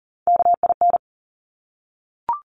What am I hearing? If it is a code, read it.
Morse code, heard as a keyed tone, about 40 wpm, 700 Hz. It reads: KIN
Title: Top Three Letter Combinations in Words 40wpm